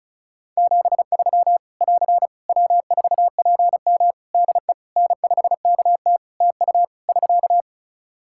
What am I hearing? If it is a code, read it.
Morse code, heard as a keyed tone, about 34 wpm, 700 Hz. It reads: 73 <AR> W4PM DE N5KT TU <SK>
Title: K1USN SST QSO Practice 35wpm